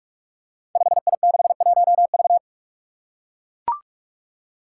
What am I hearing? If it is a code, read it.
Morse code, heard as a keyed tone, about 44 wpm, 700 Hz. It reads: 5I61V
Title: Five Challenging Letter-Number Combinations 45wpm